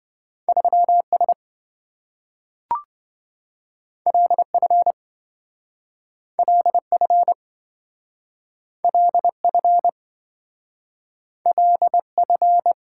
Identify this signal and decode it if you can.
Morse code, heard as a keyed tone, about 28 wpm, 700 Hz. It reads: 3S LF LF LF LF